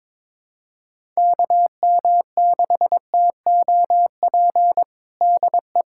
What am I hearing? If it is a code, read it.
Morse code, heard as a keyed tone, about 22 wpm, 700 Hz. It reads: KM6TOP DE
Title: USA Callsign DE Callsign - 6 Characters - 22wpm